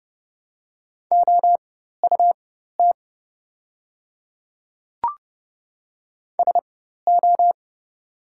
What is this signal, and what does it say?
Morse code, heard as a keyed tone, about 30 wpm, spaced as for 14 wpm, 690 Hz. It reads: OUT SO
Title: 4x Character Spacing - Top 100 Words - 30wpm